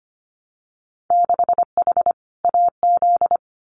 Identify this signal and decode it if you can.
Morse code, heard as a keyed tone, about 25 wpm, 700 Hz. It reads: 6H AZ
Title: Winter Field Day Exchanges Encore 25wpm